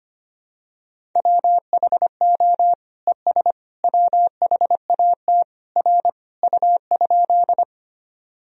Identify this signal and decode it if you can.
Morse code, heard as a keyed tone, about 25 wpm, 700 Hz. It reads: WHO ES WHAT R U?
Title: Sentences from Top 100 Words and Abbreviations 25wpm